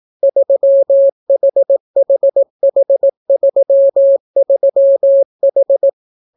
Morse code, heard as a keyed tone, about 18 wpm, 550 Hz. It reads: 3HHH33H